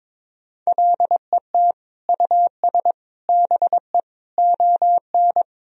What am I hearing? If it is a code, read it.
Morse code, heard as a keyed tone, about 21 wpm, 700 Hz. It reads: LET US BE ON
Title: Sentences from Top 1000 Words - Review Part 4 - 22wpm